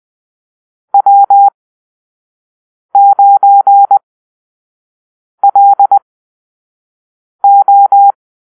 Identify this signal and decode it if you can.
Morse code, heard as a keyed tone, about 20 wpm, spaced as for 5 wpm, 800 Hz. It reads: W9LO
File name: Call-Sign Drill